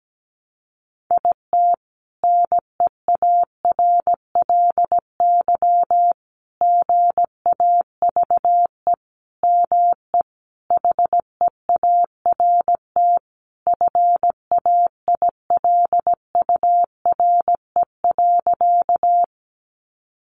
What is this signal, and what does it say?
Morse code, heard as a keyed tone, about 17 wpm, 700 Hz. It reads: IT NEARLY GAVE ME HEART FAILURE.
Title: Sentences from Top 1300 Words 17wpm